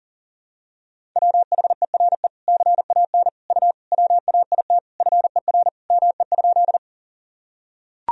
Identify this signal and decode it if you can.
Morse code, heard as a keyed tone, about 37 wpm, 700 Hz. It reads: WHERE CAN U WAIT FER ME?
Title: Sentences from Top 500 Words and Abbreviations 40wpm